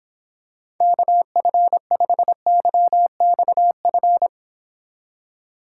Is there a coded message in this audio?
Yes: Morse code from 0.8 to 4.3 s, heard as a keyed tone, about 26 wpm, 700 Hz. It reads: KF5YXF